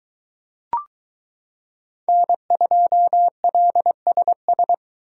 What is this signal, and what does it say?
Morse code, heard as a keyed tone, about 22 wpm, 700 Hz. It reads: N2LSS